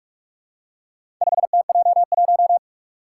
Morse code, heard as a keyed tone, about 44 wpm, 700 Hz. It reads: 5TJ1
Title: Four Challenging Letter-Number Combinations 45wpm